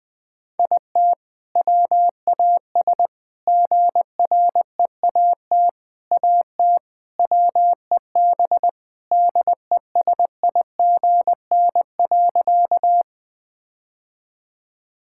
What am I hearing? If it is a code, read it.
Morse code, heard as a keyed tone, about 20 wpm, 700 Hz. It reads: IT WAS GREAT AT WEB DESIGN.
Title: All-Age Appropriate Jokes - 20wpm